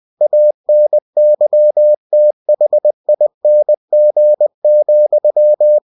Morse code, heard as a keyed tone, about 20 wpm, 600 Hz. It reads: ANYTHING,
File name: cw-books / winnie-the-pooh / winnie-20wpm-page-0013